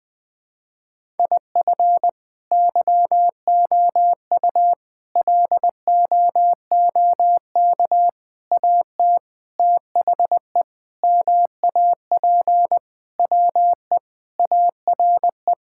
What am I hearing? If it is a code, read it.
Morse code, heard as a keyed tone, about 21 wpm, 700 Hz. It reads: IF YOU LOOK AT THE MAP WE ARE
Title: Sentences with 5-Word Phrases - Spoken Beginning - 20wpm